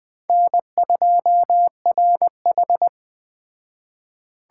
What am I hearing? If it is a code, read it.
Morse code, heard as a keyed tone, about 20 wpm, 700 Hz. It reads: N2RH
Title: Speed Racing - USA 1x2 Call Signs 17wpm